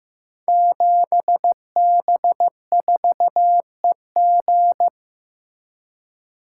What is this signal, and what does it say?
Morse code, heard as a keyed tone, about 15 wpm, 700 Hz. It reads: 7B4EG
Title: Code Groups - 5 Letter:Numbers - 15wpm